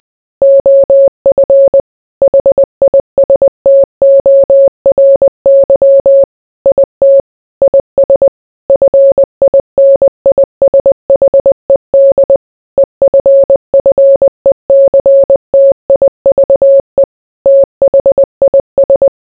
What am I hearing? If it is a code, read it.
Morse code, heard as a keyed tone, about 20 wpm, 550 Hz. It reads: OF HISTORY IT IS FINISHED EFFECTIVE THIS